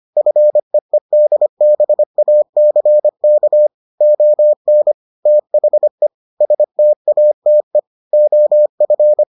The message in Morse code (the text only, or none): FEEDBACK ON THE STATE OF